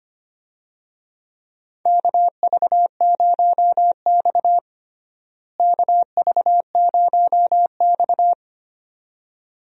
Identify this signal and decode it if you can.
Morse code, heard as a keyed tone, about 25 wpm, 700 Hz. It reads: KV0X KV0X
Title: USA Call Signs - 4 Characters - Repeated 2x - 25wpm